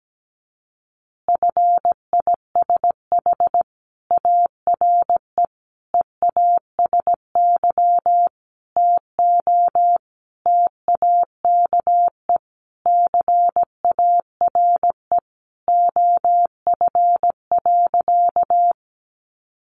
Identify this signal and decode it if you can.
Morse code, heard as a keyed tone, about 17 wpm, 700 Hz. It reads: FISH ARE EASY TO TAKE CARE OF.
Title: Additional Easy Phrases 17wpm